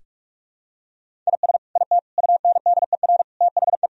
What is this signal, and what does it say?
Morse code, heard as a keyed tone, about 48 wpm, 700 Hz. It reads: IS IT UNDER THE
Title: Sentences from Top 1000 Words - Review Part 1 - 45wpm